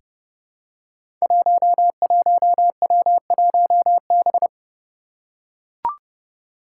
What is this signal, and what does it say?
Morse code, heard as a keyed tone, about 30 wpm, 700 Hz. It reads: 11W1B
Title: Five Challenging Letter-Number Combinations 30wpm